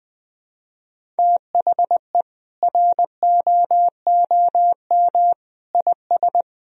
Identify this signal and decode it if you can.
Morse code, heard as a keyed tone, about 20 wpm, 700 Hz. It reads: THE ROOM IS
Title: Sentences from Top 1000 Words - Review Part 4 - 20wpm